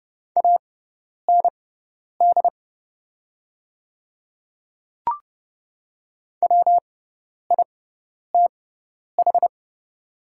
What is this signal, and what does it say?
Morse code, heard as a keyed tone, about 30 wpm, spaced as for 10 wpm, 700 Hz. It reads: AND WITH
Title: 6x Character Spacing - Top 100 Words - 30wpm